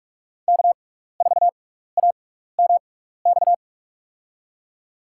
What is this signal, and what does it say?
Morse code, heard as a keyed tone, about 44 wpm, spaced as for 15 wpm, 700 Hz. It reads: K4AMX
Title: USA Call Signs - 6x Character Spacing - 45wpm